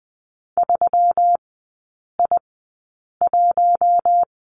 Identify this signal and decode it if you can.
Morse code, heard as a keyed tone, about 20 wpm, spaced as for 8 wpm, 700 Hz. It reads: 3I1
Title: 127 Lesson 21e - Number 6 - Three Characters 20wpm